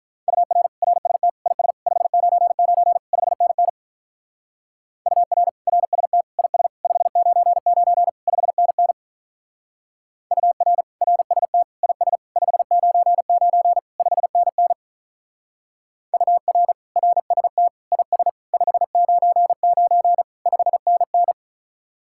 Morse code, heard as a keyed tone, about 42 wpm, 700 Hz. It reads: UR RST IS 599 5NN UR RST IS 599 5NN UR RST IS 599 5NN UR RST IS 599 5NN
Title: Speed-Racing Common QSO Phrases - 35wpm